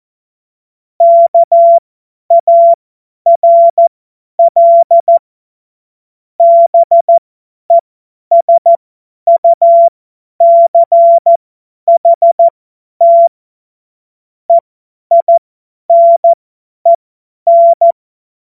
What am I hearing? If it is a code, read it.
Morse code, heard as a keyed tone, about 14 wpm, spaced as for 10 wpm, 680 Hz. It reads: KARL BESUCHT EINEN
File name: Morsekurs / Witze / Witz 04. Januar 2022, 50 F70